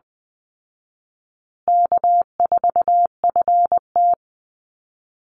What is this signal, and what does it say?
Morse code, heard as a keyed tone, about 20 wpm, 700 Hz. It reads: K4FT
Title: Call Signs Repeated 2x - 20wpm